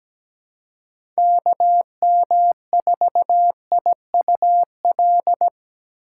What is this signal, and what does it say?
Morse code, heard as a keyed tone, about 17 wpm, 700 Hz. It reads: KM4IUL
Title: USA Call Signs - 6 Characters - 3x Character Spacing - 17wpm